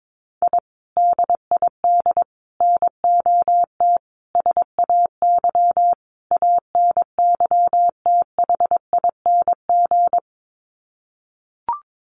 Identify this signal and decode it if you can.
Morse code, heard as a keyed tone, about 22 wpm, 700 Hz. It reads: I DID NOT SAY ANYTHING
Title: Sets of 5 Words Encore2 22wpm